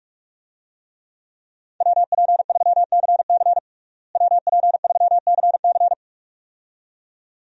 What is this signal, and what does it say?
Morse code, heard as a keyed tone, about 45 wpm, 700 Hz. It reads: WP3CC WP3CC
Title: USA Call Signs - 5 Characters - Repeated 2x - 45wpm